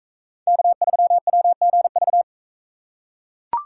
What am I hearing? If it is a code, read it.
Morse code, heard as a keyed tone, about 39 wpm, 700 Hz. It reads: K3WGV